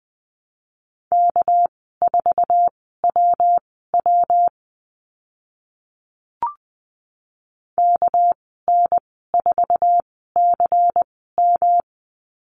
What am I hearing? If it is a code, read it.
Morse code, heard as a keyed tone, about 20 wpm, spaced as for 14 wpm, 700 Hz. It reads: K4WW KN4CM
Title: 200 Lesson 32g - Letter k - Call Signs 20wpm